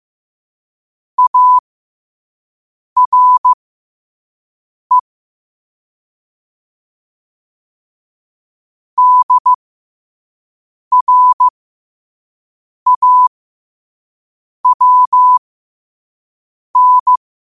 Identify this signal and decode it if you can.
Morse code, heard as a keyed tone, about 15 wpm, spaced as for 5 wpm, 1000 Hz. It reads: ARE DRAWN
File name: qotdfiles / 5wpm